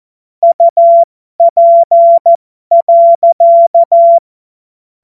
Morse code, heard as a keyed tone, about 14 wpm, spaced as for 12 wpm, 680 Hz. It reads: UP.